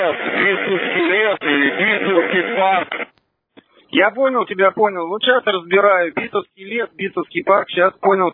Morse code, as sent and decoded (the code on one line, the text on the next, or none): none